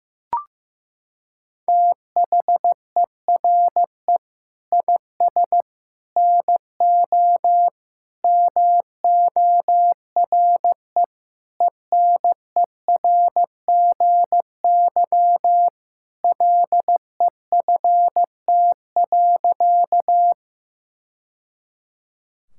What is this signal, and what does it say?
Morse code, heard as a keyed tone, about 16 wpm, 700 Hz. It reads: THERE IS NO MORE ENERGY LEFT.
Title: Sentences from Top 1000 Words - Review Part 6 - 15wpm